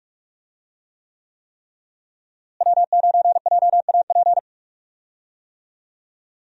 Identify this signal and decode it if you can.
Morse code, heard as a keyed tone, about 41 wpm, 700 Hz. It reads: W9JAP
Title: USA Call Signs - 5 Characters - Repeated 3x - 45wpm